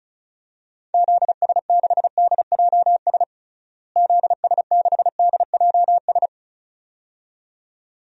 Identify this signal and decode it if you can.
Morse code, heard as a keyed tone, about 34 wpm, 700 Hz. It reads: ZS6DJS ZS6DJS
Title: DXCC-100 Call Signs - 6 Characters - Repeated 2x - 35wpm